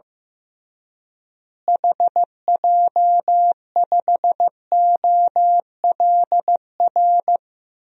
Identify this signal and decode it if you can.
Morse code, heard as a keyed tone, about 15 wpm, 700 Hz. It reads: HJ5OLR
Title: DXCC-100 Call Signs - 6 Characters - Repeated 3x - 15wpm